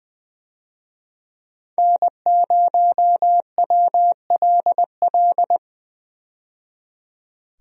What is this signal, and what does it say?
Morse code, heard as a keyed tone, about 20 wpm, 700 Hz. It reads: N0WLL
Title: USA Call Signs - 5 Characters - 20wpm